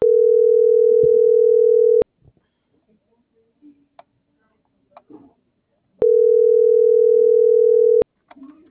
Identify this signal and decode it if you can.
Ringback tone (North America), 440 + 480 Hz.
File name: CENTRAL TERRESTRE / out karis - .30649